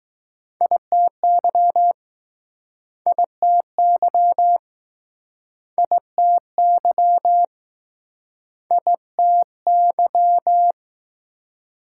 Morse code, heard as a keyed tone, about 19 wpm, 700 Hz. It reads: ITY ITY ITY ITY